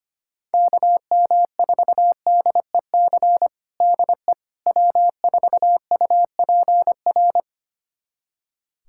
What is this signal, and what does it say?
Morse code, heard as a keyed tone, about 25 wpm, 700 Hz. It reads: KM4DEC DE W4UPR
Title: USA Callsign DE Callsign 25wpm